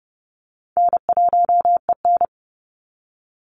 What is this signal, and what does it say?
Morse code, heard as a keyed tone, about 30 wpm, 700 Hz. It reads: N1EN
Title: Easy Call Signs - 30wpm